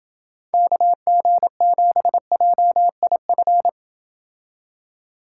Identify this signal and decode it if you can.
Morse code, heard as a keyed tone, about 29 wpm, 700 Hz. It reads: KG7JIF